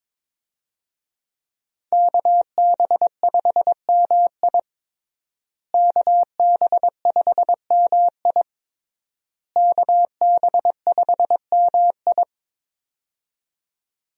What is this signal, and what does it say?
Morse code, heard as a keyed tone, about 22 wpm, 700 Hz. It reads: KB5MI KB5MI KB5MI
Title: USA Call Signs - 5 Characters - Repeated 3x - 22wpm